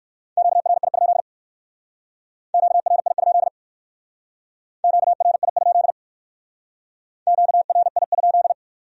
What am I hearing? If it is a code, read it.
Morse code, heard as a keyed tone, about 57 wpm, 700 Hz. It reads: QRI? QRI? QRI? QRI?